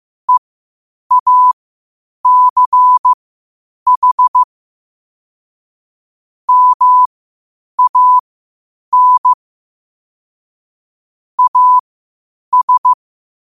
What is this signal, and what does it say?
Morse code, heard as a keyed tone, about 15 wpm, spaced as for 8 wpm, 1000 Hz. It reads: EACH MAN AS
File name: qotdfiles / 8wpm